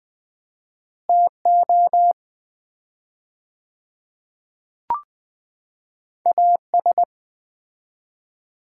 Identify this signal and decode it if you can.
Morse code, heard as a keyed tone, about 20 wpm, 700 Hz. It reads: TO AS
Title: 4x Word-Spacing - Sets of 2 Words - 20wpm